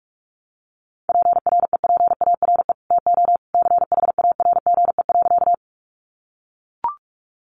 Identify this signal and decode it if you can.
Morse code, heard as a keyed tone, about 44 wpm, 700 Hz. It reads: PREPARE TO CHARGE.
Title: Sentences from Top 900 Words 45wpm